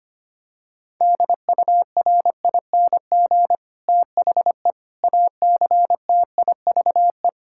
Morse code, heard as a keyed tone, about 25 wpm, 700 Hz. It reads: DURING THE ACTIVE
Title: Sentences from Top 1000 Words - Review Part 6 - 25wpm